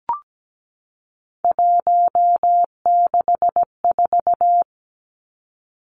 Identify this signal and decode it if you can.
Morse code, heard as a keyed tone, about 17 wpm, 700 Hz. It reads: 164
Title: Numbers - 3-Digit 17wpm